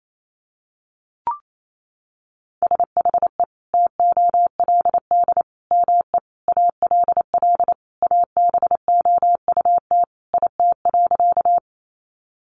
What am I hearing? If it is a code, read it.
Morse code, heard as a keyed tone, about 28 wpm, 700 Hz. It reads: SHE TOLD ME ALL ABOUT IT.
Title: Sentences from Top 400 Words 28wpm